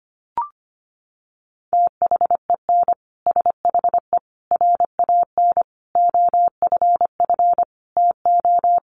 Morse code, heard as a keyed tone, about 25 wpm, 700 Hz. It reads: THEN SHE RAN OFF TO
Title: Sentences from Top 1400 Words 25wpm